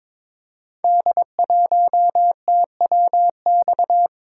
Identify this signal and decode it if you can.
Morse code, heard as a keyed tone, about 22 wpm, 700 Hz. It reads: D1TWX
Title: Code Groups - 5 Letter:Numbers - 22wpm